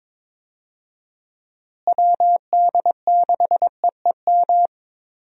Morse code, heard as a keyed tone, about 22 wpm, 700 Hz. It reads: WD6EEM
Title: USA Call Signs - 6 Characters - Repeated 2x - 22wpm